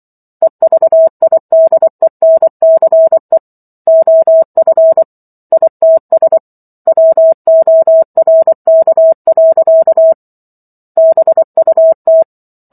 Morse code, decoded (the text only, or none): EVIDENCE OF ITS WORK. BUT